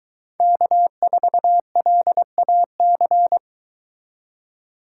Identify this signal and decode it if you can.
Morse code, heard as a keyed tone, about 23 wpm, 700 Hz. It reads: K4LAC